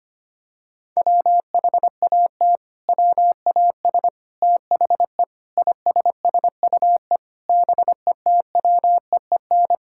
Morse code, heard as a keyed tone, about 25 wpm, 700 Hz. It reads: WHAT WAS THE ISSUE BETWEEN
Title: Sentences from Top 1000 Words - Review Part 6 - 25wpm